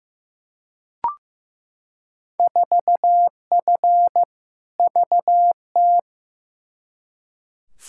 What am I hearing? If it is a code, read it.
Morse code, heard as a keyed tone, about 15 wpm, 700 Hz. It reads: 4F VT
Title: Field Day Exchanges Encore - 15wpm